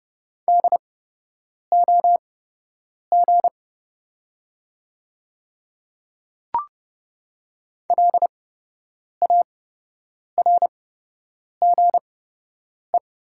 Morse code, heard as a keyed tone, about 30 wpm, spaced as for 8 wpm, 700 Hz. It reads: DOG LARGE
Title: 8x Character Spacing - Top 500 Words - 30wpm